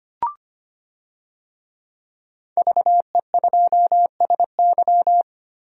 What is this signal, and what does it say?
Morse code, heard as a keyed tone, about 25 wpm, 700 Hz. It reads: VE2SY
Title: DXCC-100 Call Signs - 5 Characters - 25wpm